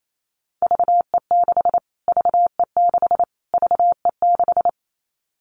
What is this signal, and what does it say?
Morse code, heard as a keyed tone, about 28 wpm, 700 Hz. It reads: VE6 VE6 VE6
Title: Callsign Prefixes for Top 25 DXCC - Repeated 3x - 28wpm